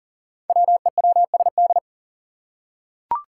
Morse code, heard as a keyed tone, about 40 wpm, 700 Hz. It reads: WEWSD
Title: Five Challenging Letter-Number Combinations 40wpm